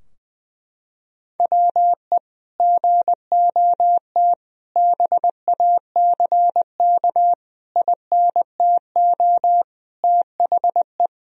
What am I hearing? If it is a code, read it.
Morse code, heard as a keyed tone, about 20 wpm, 700 Hz. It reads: WE GOT BACK INTO THE
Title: Sentences from Top 1000 Words - Review Part 3 - 20wpm